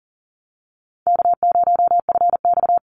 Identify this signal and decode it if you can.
Morse code, heard as a keyed tone, about 40 wpm, 700 Hz. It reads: K0FX
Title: Easy Call Signs Repeated 2x - 40wpm